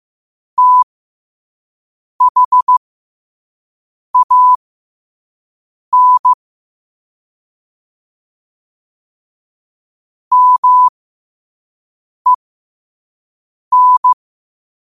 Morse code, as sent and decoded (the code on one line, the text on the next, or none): - .... .- -. / -- . -.
THAN MEN